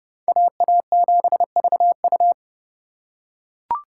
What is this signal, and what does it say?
Morse code, heard as a keyed tone, about 28 wpm, 700 Hz. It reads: AA7VU